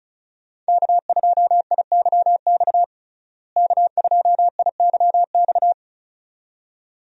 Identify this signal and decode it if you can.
Morse code, heard as a keyed tone, about 34 wpm, 700 Hz. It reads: K2IYX K2IYX
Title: USA Call Signs - 5 Characters - Repeated 2x - 35wpm